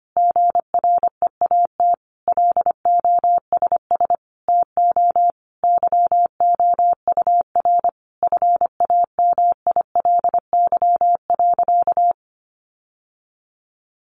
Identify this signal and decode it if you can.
Morse code, heard as a keyed tone, about 25 wpm, 700 Hz. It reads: GREAT LOSS TO YOUR FAMILY.
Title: Sentences from Top 1400 Words 25wpm